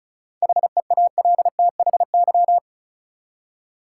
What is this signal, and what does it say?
Morse code, heard as a keyed tone, about 36 wpm, 700 Hz. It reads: HEALTHY
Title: Seven Letter Words - 35wpm